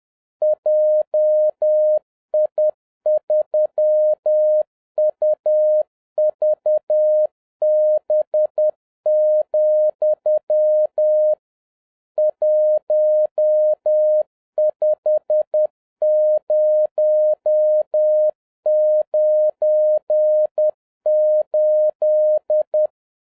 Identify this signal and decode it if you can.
Morse code, heard as a keyed tone, about 10 wpm, 610 Hz. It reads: JI3UVB, 15098